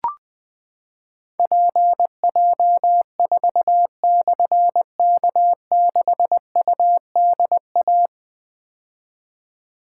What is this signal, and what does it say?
Morse code, heard as a keyed tone, about 20 wpm, 700 Hz. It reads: PJ4/K6UDA
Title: Difficult Call Signs 20wpm